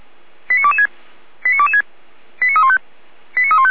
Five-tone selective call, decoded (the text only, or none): 89226 89226 89315 89315